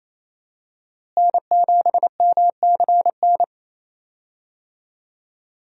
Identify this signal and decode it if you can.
Morse code, heard as a keyed tone, about 28 wpm, 700 Hz. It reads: N7MCN
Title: USA Call Signs - 5 Characters - Repeated 2x - 28wpm